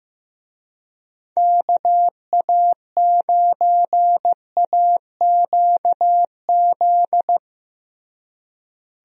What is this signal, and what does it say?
Morse code, heard as a keyed tone, about 15 wpm, 700 Hz. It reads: KA9AQZ
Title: USA Call Signs - 6 Characters - Repeated 3x - 15wpm